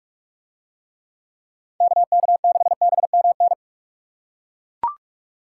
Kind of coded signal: Morse code

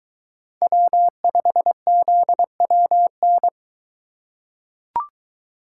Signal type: Morse code

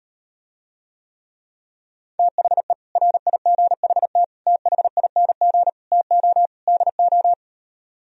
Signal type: Morse code

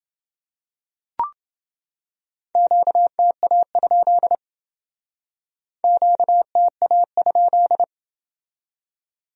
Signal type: Morse code

